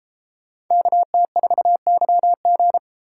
Morse code, heard as a keyed tone, about 32 wpm, 700 Hz. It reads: KT4YG